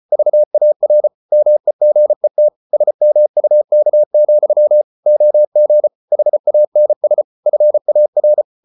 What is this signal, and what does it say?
Morse code, heard as a keyed tone, about 34 wpm, 600 Hz. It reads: VAR MEGET SMUK, OG HANS FAR
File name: Ferie QRQ 34wpm